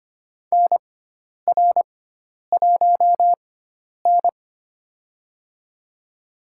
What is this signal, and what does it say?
Morse code, heard as a keyed tone, about 25 wpm, spaced as for 10 wpm, 700 Hz. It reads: NR1N
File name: Lesson 16g - Letter c - Call Signs 25wpm